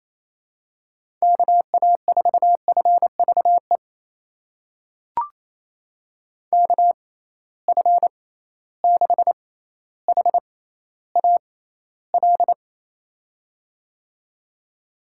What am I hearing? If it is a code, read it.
Morse code, heard as a keyed tone, about 28 wpm, spaced as for 14 wpm, 700 Hz. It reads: KA4FVE KF6HAL
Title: USA Call Signs - 6 Characters - 6x Character Spacing - 28wpm